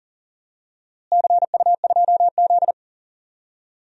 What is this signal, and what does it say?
Morse code, heard as a keyed tone, about 37 wpm, 700 Hz. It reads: CU2Z
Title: DXCC-100 Call Signs - 4 Characters - Repeated 2x - 40wpm